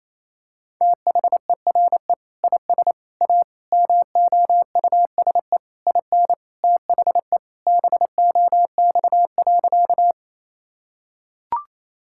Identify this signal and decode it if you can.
Morse code, heard as a keyed tone, about 28 wpm, 700 Hz. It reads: THERE IS A MOUSE IN THE BOX.
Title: Sentences from Top 1000 Words - Review Part 5 - 28wpm